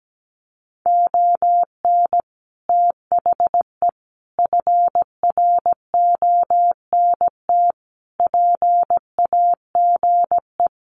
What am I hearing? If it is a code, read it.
Morse code, heard as a keyed tone, about 17 wpm, 700 Hz. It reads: ON THE FRONT PAGE
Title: Sets of 5 Words Encore 17wpm